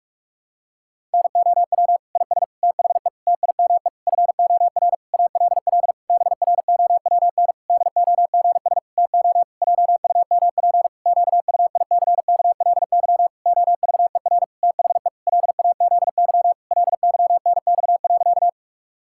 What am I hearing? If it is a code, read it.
Morse code, heard as a keyed tone, about 46 wpm, 700 Hz. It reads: NOW IS THE TIME FOR ALL BROWN DOGS TO JUMP QUICKLY OVER THE LAZY LYNX.
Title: Pangram Challenge v2 - 45wpm